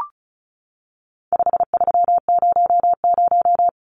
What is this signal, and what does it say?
Morse code, heard as a keyed tone, about 34 wpm, 700 Hz. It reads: 5300
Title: Numbers - 4-Digit 35wpm